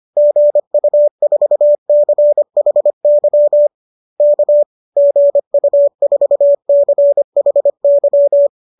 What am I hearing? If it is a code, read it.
Morse code, heard as a keyed tone, about 25 wpm, 590 Hz. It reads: GU4CHY K GU4CHY
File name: qso210 25